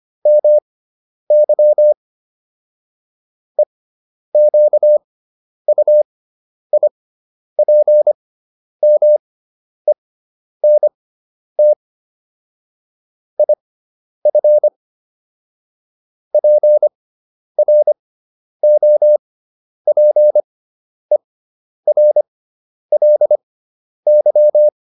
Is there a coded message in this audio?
Yes: Morse code from 0.3 to 24.7 s, heard as a keyed tone, about 25 wpm, spaced as for 10 wpm, 600 Hz. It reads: MY EQUIPMENT IF PROPERLY